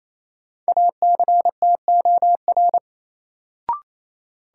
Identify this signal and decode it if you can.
Morse code, heard as a keyed tone, about 28 wpm, 700 Hz. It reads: ACTOR
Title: Sentences from Top 1500 Words - New Word Drill - 28wpm